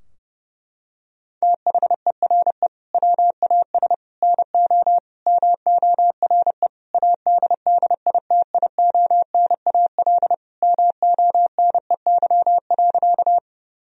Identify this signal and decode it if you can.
Morse code, heard as a keyed tone, about 32 wpm, 700 Hz. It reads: THERE WAS NO MORE ADDITIONAL MONEY.
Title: Sentences from Top 1000 Words - Review Part 4 - 30wpm